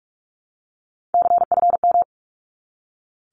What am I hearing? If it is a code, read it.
Morse code, heard as a keyed tone, about 44 wpm, 700 Hz. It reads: CFM
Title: Abbreviations 45wpm